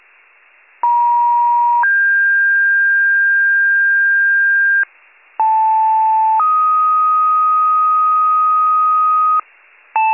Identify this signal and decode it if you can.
Two-tone page, heard as a one-second tone, then a three-second tone, about 950 then 1600 Hz, about 880 then 1250 Hz, with noise. It is heard from 0.8 to 9.4 s.